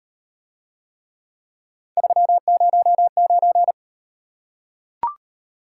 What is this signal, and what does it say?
Morse code, heard as a keyed tone, about 38 wpm, 700 Hz. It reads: 309